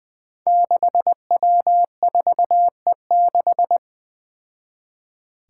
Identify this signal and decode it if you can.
Morse code, heard as a keyed tone, about 20 wpm, 700 Hz. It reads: 6W4E6
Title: Five Challenging Letter-Number Combinations 20wpm